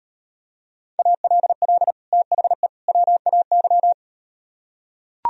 Morse code, heard as a keyed tone, about 38 wpm, 700 Hz. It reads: ALL THE WAY